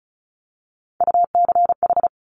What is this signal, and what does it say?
Morse code, heard as a keyed tone, about 36 wpm, 700 Hz. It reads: UCH
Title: Top Three Letter Combinations in Words 35wpm